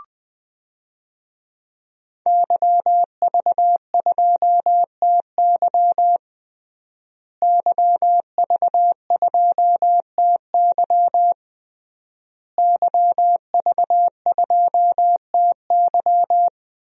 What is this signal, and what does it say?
Morse code, heard as a keyed tone, about 20 wpm, 700 Hz. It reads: YV2TY YV2TY YV2TY
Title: DXCC-100 Call Signs - 5 Characters - Repeated 3x - 20wpm